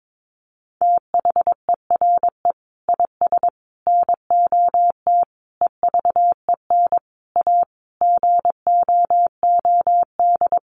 Morse code, heard as a keyed tone, about 22 wpm, 700 Hz. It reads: THERE IS NOT EVEN A GOOD
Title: Sentences from Top 1500 Words 22wpm